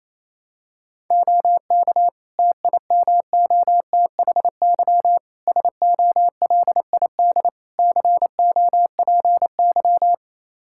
Morse code, heard as a keyed tone, about 28 wpm, 700 Hz. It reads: OK TIMOTHY SOLID COPY
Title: K1USN SST QSO Practice 28wpm